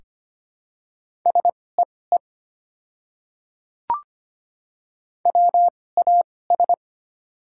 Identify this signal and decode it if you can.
Morse code, heard as a keyed tone, about 25 wpm, spaced as for 18 wpm, 690 Hz. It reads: SEE WAS
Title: 2x Character Spacing - Top 100 Words - 25wpm